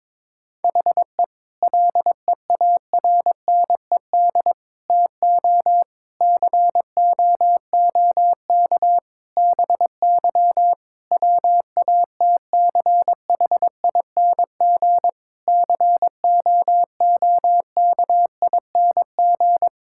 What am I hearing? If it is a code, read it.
Morse code, heard as a keyed tone, about 23 wpm, 700 Hz. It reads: HE LEARNED TO COOK BY WATCHING COOKING
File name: Difficult Sentences with Hints - 22wpm